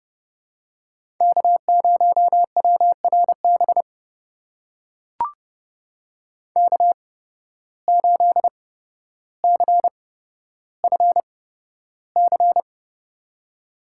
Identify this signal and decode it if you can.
Morse code, heard as a keyed tone, about 32 wpm, spaced as for 13 wpm, 700 Hz. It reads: K0WRB K8CFC
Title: USA Call Signs - 5 Characters - 8x Character Spacing - 30wpm